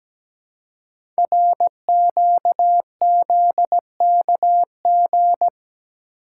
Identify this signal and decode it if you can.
Morse code, heard as a keyed tone, about 17 wpm, 700 Hz. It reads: RQZKG
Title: Code Groups - 5 Letter:Numbers - 17wpm